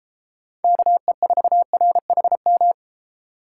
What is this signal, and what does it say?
Morse code, heard as a keyed tone, about 31 wpm, 700 Hz. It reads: KE4RHM